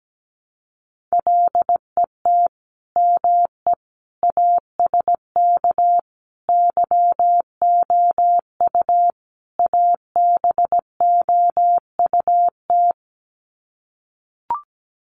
Morse code, heard as a keyed tone, about 17 wpm, 700 Hz. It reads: LET ME ASK YOU ABOUT
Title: Sets of 5 Words 17wpm